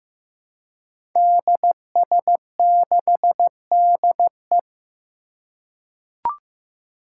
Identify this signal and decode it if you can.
Morse code, heard as a keyed tone, about 15 wpm, 700 Hz. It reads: DS6DE
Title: Five Challenging Letter-Number Combinations 15wpm